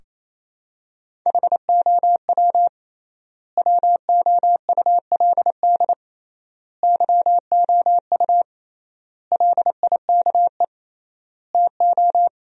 Morse code, heard as a keyed tone, about 28 wpm, 700 Hz. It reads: HOW WOULD YOU LIKE TO
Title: 3x Word-Spacing - Sentences from Top 100 Words - 28wpm